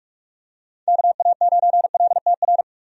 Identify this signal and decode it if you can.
Morse code, heard as a keyed tone, about 46 wpm, 700 Hz. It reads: KA9LTR
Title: USA Call Signs - 6 Characters - 45wpm